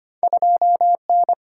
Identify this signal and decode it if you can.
Morse code, heard as a keyed tone, about 25 wpm, 700 Hz. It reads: 2N
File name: Two Letter-Number Combinations - 25wpm